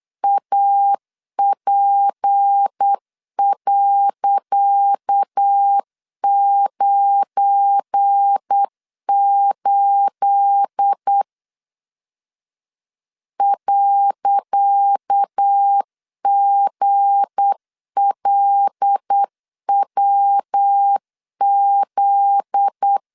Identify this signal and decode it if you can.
Morse code, heard as a keyed tone, about 8 wpm, 800 Hz. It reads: AP.98 .GLWZ
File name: LEZIONI / 10 SIGNS / 10C DOT 8wpm 3649